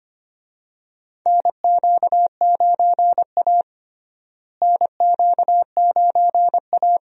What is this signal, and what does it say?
Morse code, heard as a keyed tone, about 25 wpm, 700 Hz. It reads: NQ9A NQ9A
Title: USA Call Signs - 4 Characters - Repeated 2x - 25wpm